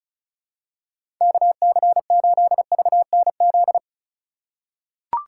Morse code, heard as a keyed tone, about 34 wpm, 700 Hz. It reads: KC8VNZ
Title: USA Call Signs - 6x Character Spacing - 35wpm